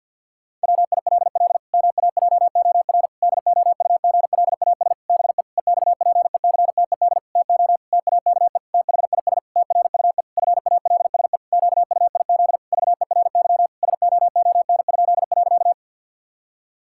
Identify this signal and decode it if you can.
Morse code, heard as a keyed tone, about 50 wpm, 700 Hz. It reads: WILL MAJOR DOUGLAS BE EXPECTED TO TAKE THIS TRUE FALSE QUIZ VERY SOON?.
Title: Pangram Challenge v2 - 50wpm